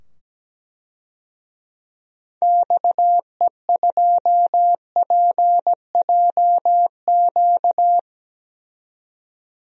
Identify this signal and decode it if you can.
Morse code, heard as a keyed tone, about 17 wpm, 700 Hz. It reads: XE2PJQ
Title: DXCC-100 Call Signs - 6 Characters - 17wpm